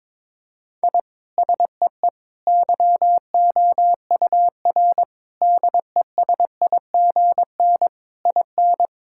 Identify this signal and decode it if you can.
Morse code, heard as a keyed tone, about 22 wpm, 700 Hz. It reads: I SEE YOUR DESIGN IN
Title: Sentences from Top 1000 Words - Review Part 5 - 22wpm